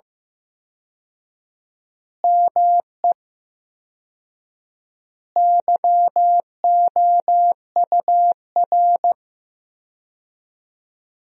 Morse code, heard as a keyed tone, about 15 wpm, 700 Hz. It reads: ME YOUR
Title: 4x Word-Spacing - Sentences from Top 200 Words - 15wpm